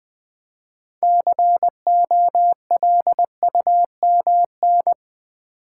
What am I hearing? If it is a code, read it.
Morse code, heard as a keyed tone, about 20 wpm, 700 Hz. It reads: COLUMN
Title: 6 Letter Words - First 3 Letters Given - 20wpm